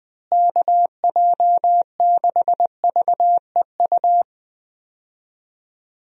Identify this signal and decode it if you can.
Morse code, heard as a keyed tone, about 20 wpm, 700 Hz. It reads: KJ6VEU
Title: USA Call Signs - 6 Characters - Repeated 3x - 20wpm